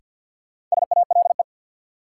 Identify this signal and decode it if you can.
Morse code, heard as a keyed tone, about 53 wpm, 700 Hz. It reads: SALE
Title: Sentences from Top 1400 Words - New Word Drill Alt - 50wpm